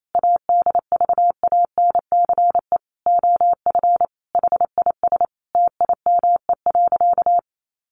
Morse code, heard as a keyed tone, about 28 wpm, 700 Hz. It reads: ADVANCE OF HIS TIME.
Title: Sentences from Top 1400 Words 28wpm